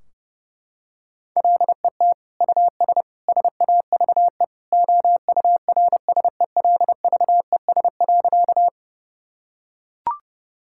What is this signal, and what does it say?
Morse code, heard as a keyed tone, about 31 wpm, 700 Hz. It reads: LET US SAVE OURSELVES.
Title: Sentences from Top 1600 Words 30wpm